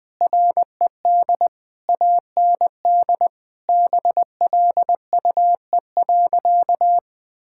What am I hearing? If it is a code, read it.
Morse code, heard as a keyed tone, about 20 wpm, 700 Hz. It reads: RED AND BLUE.
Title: Sentences with 6-Word Phrases - Spoken Beginning - 20wpm